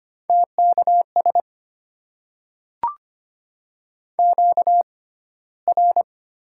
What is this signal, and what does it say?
Morse code, heard as a keyed tone, about 25 wpm, 700 Hz. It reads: T K S Q R
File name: Top 90 QSO Elements - 6x Character Spacing - 25wpm